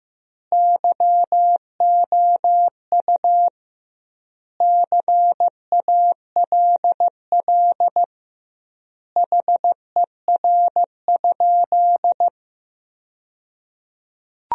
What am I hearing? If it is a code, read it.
Morse code, heard as a keyed tone, about 15 wpm, 700 Hz. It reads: YOU CALL HER?
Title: 2x Word-Spacing - Sentences from Top 100 Words - 15wpm